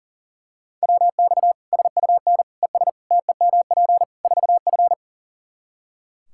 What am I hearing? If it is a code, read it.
Morse code, heard as a keyed tone, about 37 wpm, 700 Hz. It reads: WX SUN ES TEMP 4F
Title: Practice QSO - Exchange 2 - Info - 40wpm